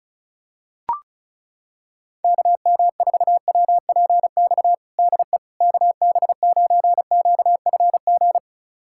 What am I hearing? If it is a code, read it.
Morse code, heard as a keyed tone, about 34 wpm, 700 Hz. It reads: KM4WPX DE KB9QFG
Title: USA Callsign DE Callsign - 6 Characters - 35wpm